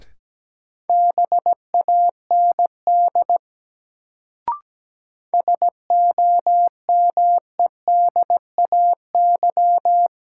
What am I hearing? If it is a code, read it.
Morse code, heard as a keyed tone, about 17 wpm, 700 Hz. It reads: BAND SOMEDAY
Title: Sentences from Top 1600 Words - New Word Drill Alt - 17wpm